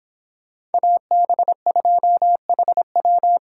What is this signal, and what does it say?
Morse code, heard as a keyed tone, about 31 wpm, 700 Hz. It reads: AB2HW